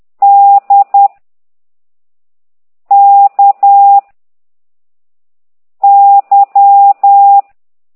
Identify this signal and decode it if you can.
Morse code, heard as a keyed tone, about 10 wpm, 800 Hz. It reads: D K Y
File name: ct2idw / exerc15